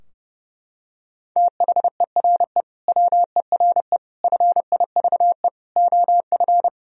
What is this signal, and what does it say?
Morse code, heard as a keyed tone, about 30 wpm, 700 Hz. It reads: THERE WERE FIVE OF
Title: Sentences from Top 1000 Words - Review Part 2 - 30wpm